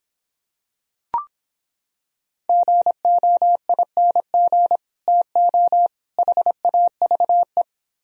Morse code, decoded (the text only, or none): GOING TO HAVE